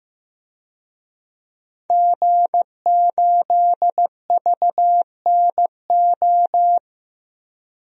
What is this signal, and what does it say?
Morse code, heard as a keyed tone, about 15 wpm, 700 Hz. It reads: G8VNO
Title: DXCC-100 Call Signs - 5 Characters - 15wpm